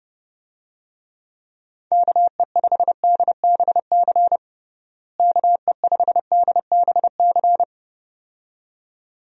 Morse code, heard as a keyed tone, about 29 wpm, 700 Hz. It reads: KE5DBC KE5DBC
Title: USA Call Signs - 6 Characters - Repeated 2x - 30wpm